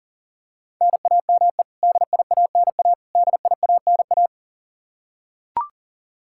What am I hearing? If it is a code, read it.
Morse code, heard as a keyed tone, about 40 wpm, 700 Hz. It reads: NAME DIANA DIANA
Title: Practice QSO - Exchange 1 - Beginning - 40wpm